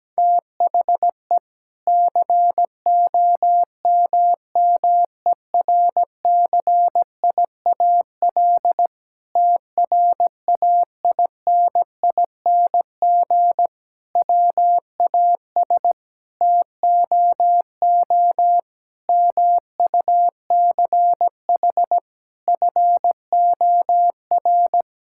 Morse code, heard as a keyed tone, about 17 wpm, 700 Hz. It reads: THE COMMERCIAL TRAINING WAS TOO MUCH FOR
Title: Sentences from Top 1000 Words - Review Part 5 - 17wpm